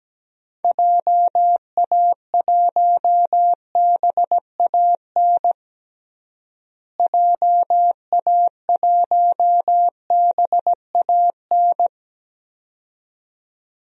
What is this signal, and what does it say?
Morse code, heard as a keyed tone, about 17 wpm, 700 Hz. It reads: JA1BAN JA1BAN
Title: DXCC-100 Call Signs - 6 Characters - Repeated 2x - 17wpm